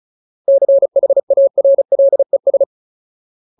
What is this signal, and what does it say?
Morse code, heard as a keyed tone, about 36 wpm, 550 Hz. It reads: CHARLES
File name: Ultimate Word List / 35WPM Ultimate